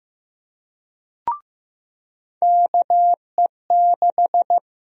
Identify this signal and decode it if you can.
Morse code, heard as a keyed tone, about 16 wpm, 700 Hz. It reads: KE6
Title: USA Call Sign Prefixes - Repeated 2x - 15wpm